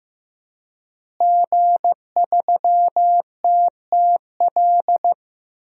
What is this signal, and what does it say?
Morse code, heard as a keyed tone, about 15 wpm, 700 Hz. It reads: G3TTL
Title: DXCC-100 Call Signs - 5 Characters - Repeated 3x - 15wpm